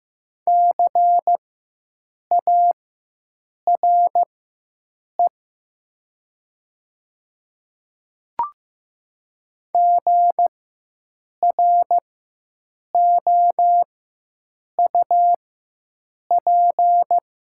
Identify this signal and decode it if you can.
Morse code, heard as a keyed tone, about 15 wpm, spaced as for 7 wpm, 700 Hz. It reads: CARE GROUP
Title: 4x Character Spacing - Top 300 Words - 15wpm